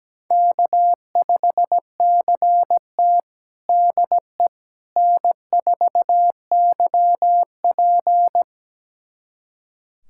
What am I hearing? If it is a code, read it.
Morse code, heard as a keyed tone, about 17 wpm, 700 Hz. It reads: K5CT DE N4YP
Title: USA Callsign DE Callsign - 4 Characters - 17wpm